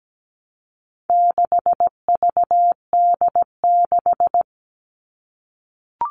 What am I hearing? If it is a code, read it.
Morse code, heard as a keyed tone, about 17 wpm, 700 Hz. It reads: 6VD6
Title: All Cut Numbers - 4-Digit 17wpm